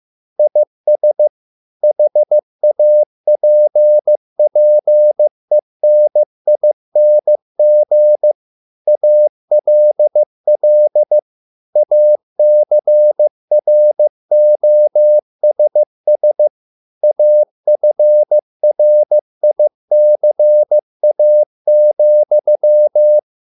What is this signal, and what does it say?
Morse code, heard as a keyed tone, about 15 wpm, 600 Hz. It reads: IS HAPPENING ALL ACROSS AFRICA,